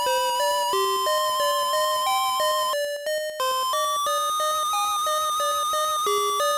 There is no code.